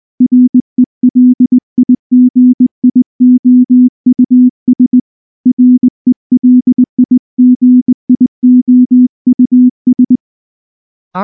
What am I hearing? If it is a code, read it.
Morse code, heard as a keyed tone, about 19 wpm, 250 Hz. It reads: RELIGIOUS RELIGIOUS
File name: morse / morse-training-20WPM-5